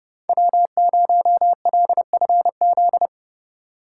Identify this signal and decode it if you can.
Morse code, heard as a keyed tone, about 37 wpm, 700 Hz. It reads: W0LFZ
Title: Speed Racing - USA 1x3 Call Signs 30wpm